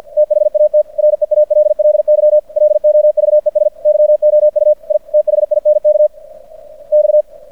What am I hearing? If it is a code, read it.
Morse code, heard as a keyed tone, about 54 wpm, 600 Hz. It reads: THAT REALLY LOVES YOU I THINK X
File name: archived / CRSnet